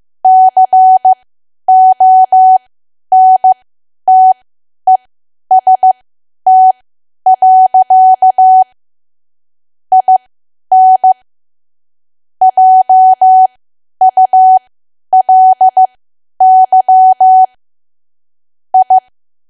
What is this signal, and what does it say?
Morse code, heard as a keyed tone, about 15 wpm, spaced as for 10 wpm, 750 Hz. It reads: CONTEST. IN JULY I
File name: Archive / 10 WPM / 10WPM